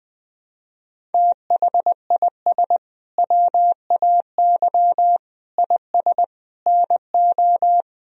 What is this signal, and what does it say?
Morse code, heard as a keyed tone, about 20 wpm, 700 Hz. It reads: THIS WAY IS NO
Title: Sentences from Top 1000 Words - Review Part 2 - 20wpm